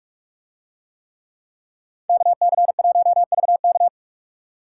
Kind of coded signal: Morse code

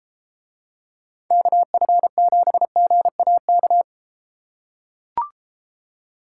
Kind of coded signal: Morse code

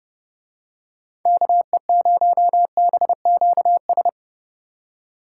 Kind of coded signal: Morse code

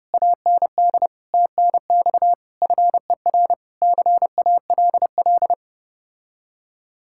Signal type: Morse code

